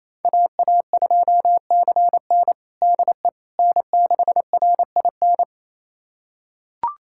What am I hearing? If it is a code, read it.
Morse code, heard as a keyed tone, about 28 wpm, 700 Hz. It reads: AA2CN DE N6RIN
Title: USA Callsign DE Callsign - 5 Characters - 28wpm